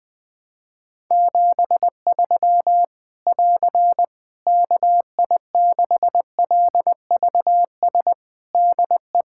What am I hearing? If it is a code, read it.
Morse code, heard as a keyed tone, about 20 wpm, 700 Hz. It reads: 73 <AR> KI6LVS DE
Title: K1USN SST QSO Practice 20wpm